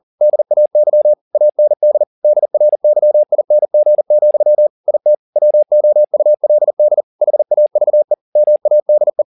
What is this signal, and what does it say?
Morse code, heard as a keyed tone, about 40 wpm, 600 Hz. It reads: DAY AND DRYING, IT WOULD HAVE MADE
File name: the old man and the sea 2 40wpm